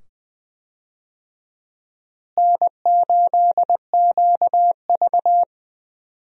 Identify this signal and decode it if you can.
Morse code, heard as a keyed tone, about 20 wpm, 700 Hz. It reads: N8QV
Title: USA Call Signs - 4 Characters - Repeated 2x - 20wpm